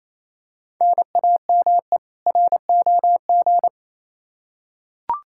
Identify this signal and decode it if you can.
Morse code, heard as a keyed tone, about 28 wpm, 700 Hz. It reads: NAME ROG
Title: Practice QSO - Exchange 1 - Beginning - 28wpm